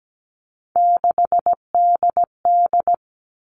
Morse code, heard as a keyed tone, about 17 wpm, 700 Hz. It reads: 6DD
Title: All Cut Numbers - 3-Digit 17wpm